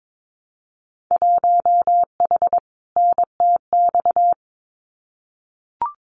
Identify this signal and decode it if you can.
Morse code, heard as a keyed tone, about 22 wpm, 700 Hz. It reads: 1H NTX
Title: Winter Field Day Exchanges 22wpm